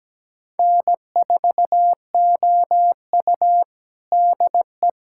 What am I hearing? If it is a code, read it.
Morse code, heard as a keyed tone, about 17 wpm, 700 Hz. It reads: N4OU DE
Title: USA Callsign DE Callsign - 4 Characters - 17wpm